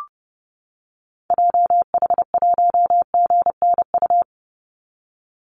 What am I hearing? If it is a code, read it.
Morse code, heard as a keyed tone, about 31 wpm, 700 Hz. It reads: JH1GNU
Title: Call Signs - 30wpm